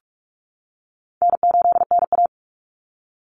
Morse code, heard as a keyed tone, about 46 wpm, 700 Hz. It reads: N8NA
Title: Easy Call Signs - 45wpm